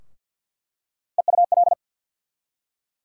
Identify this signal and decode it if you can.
Morse code, heard as a keyed tone, about 47 wpm, 700 Hz. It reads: EUL
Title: USA Call Sign Postfixes - 3 Letters 50wpm